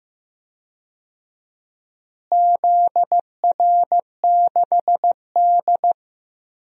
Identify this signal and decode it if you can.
Morse code, heard as a keyed tone, about 15 wpm, 700 Hz. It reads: ZR6D
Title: DXCC-100 Call Signs - 4 Characters - Repeated 2x - 15wpm